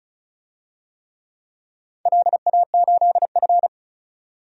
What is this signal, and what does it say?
Morse code, heard as a keyed tone, about 36 wpm, 700 Hz. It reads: LA8F
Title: DXCC-100 Call Signs - 4 Characters - 35wpm